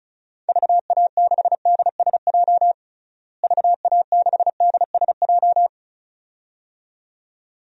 Morse code, heard as a keyed tone, about 36 wpm, 700 Hz. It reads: VA6DSJ VA6DSJ
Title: DXCC-100 Call Signs - 6 Characters - Repeated 2x - 35wpm